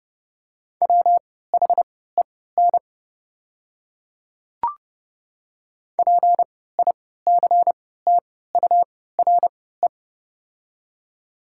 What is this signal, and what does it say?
Morse code, heard as a keyed tone, about 30 wpm, spaced as for 17 wpm, 690 Hz. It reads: WHEN PICTURE
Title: 3x Character Spacing - Top 200 Words - 30wpm